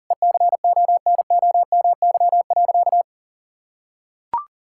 Morse code, heard as a keyed tone, about 40 wpm, 700 Hz. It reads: ECONOMY.
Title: Sentences from Top 1000 Words - Review Part 6 - 40wpm